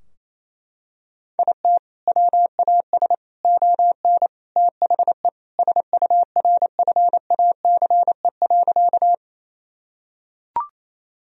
Morse code, heard as a keyed tone, about 28 wpm, 700 Hz. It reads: IT WAS ON THE SURFACE.
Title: Sentences from Top 1000 Words - Review Part 3 - 28wpm